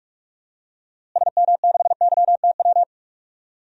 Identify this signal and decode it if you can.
Morse code, heard as a keyed tone, about 46 wpm, 700 Hz. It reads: SM6YTW
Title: DXCC-100 Call Signs - 6 Characters - Repeated 2x - 45wpm